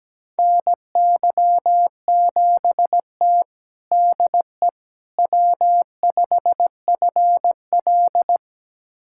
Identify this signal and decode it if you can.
Morse code, heard as a keyed tone, about 17 wpm, 700 Hz. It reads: NY7T DE W5FL
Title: USA Callsign DE Callsign - 4 Characters - 17wpm